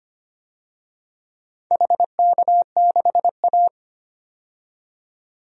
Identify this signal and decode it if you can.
Morse code, heard as a keyed tone, about 25 wpm, 700 Hz. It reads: HK6A
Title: DXCC-100 Call Signs - 4 Characters - 25wpm